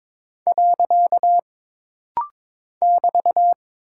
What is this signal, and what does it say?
Morse code, heard as a keyed tone, about 22 wpm, 700 Hz. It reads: . <BT>
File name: practice / 22wpm / Single Punctuation-Prosign - Rapid-Fire - 22wpm